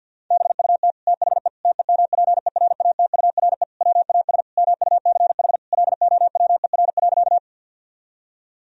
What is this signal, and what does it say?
Morse code, heard as a keyed tone, about 50 wpm, 700 Hz. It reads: BUT THE TEMPERATURE WAS MUCH LOWER.
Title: Sentences from Top 1000 Words - Review Part 4 - 50wpm